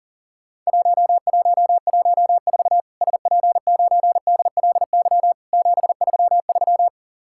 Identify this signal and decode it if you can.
Morse code, heard as a keyed tone, about 38 wpm, 700 Hz. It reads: 1114 SP9DLY 733
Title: Worked All Europe DX Contest - QTC Exchange - 40wpm